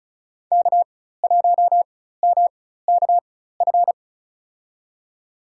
Morse code, heard as a keyed tone, about 36 wpm, spaced as for 16 wpm, 700 Hz. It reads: K1MKF
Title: USA Call Signs - 5 Characters - 4x Character Spacing - 35wpm